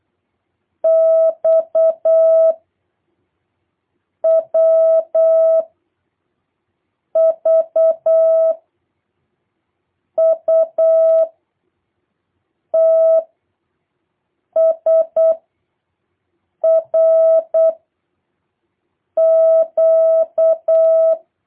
Morse code, heard as a keyed tone, about 8 wpm, 650 Hz. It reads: X W V U T S R Q